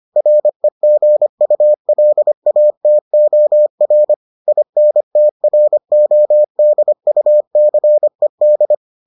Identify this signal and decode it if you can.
Morse code, heard as a keyed tone, about 25 wpm, 600 Hz. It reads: REGULATOR INTRODUCED